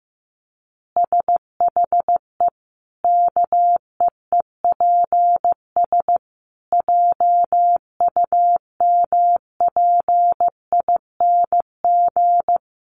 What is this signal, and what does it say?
Morse code, heard as a keyed tone, about 15 wpm, 700 Hz. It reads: SHE KEEPS JUMPING
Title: Easy Phrases 15wpm